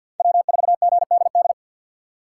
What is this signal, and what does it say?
Morse code, heard as a keyed tone, about 52 wpm, 700 Hz. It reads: W4GDD
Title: USA Call Signs - 5 Characters - 3x Character Spacing - 50wpm